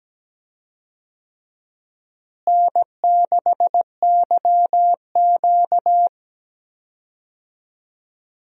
Morse code, heard as a keyed tone, about 17 wpm, 700 Hz. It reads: N6YQ
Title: USA Call Signs - 4 Characters - Repeated 3x - 17wpm